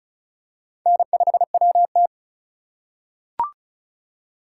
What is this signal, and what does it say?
Morse code, heard as a keyed tone, about 36 wpm, 690 Hz. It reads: N5WT
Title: USA Call Signs - 4 Characters - 2x Character Spacing - 35wpm